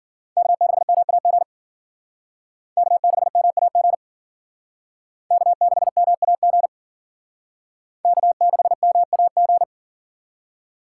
Morse code, heard as a keyed tone, about 50 wpm, 700 Hz. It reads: K6MAG K6MAG K6MAG K6MAG
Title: Speed Racing - USA 1x3 Call Signs 40wpm